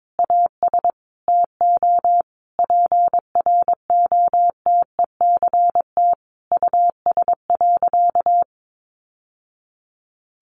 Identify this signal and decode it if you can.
Morse code, heard as a keyed tone, about 22 wpm, 700 Hz. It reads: AS TO PROTECT US.
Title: Sentences from Top 900 Words 22wpm